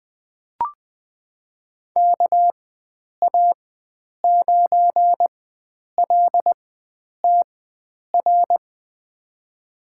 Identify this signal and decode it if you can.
Morse code, heard as a keyed tone, about 21 wpm, spaced as for 9 wpm, 700 Hz. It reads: KA9LTR
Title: USA Call Signs - 6 Characters - 4x Character Spacing - 20wpm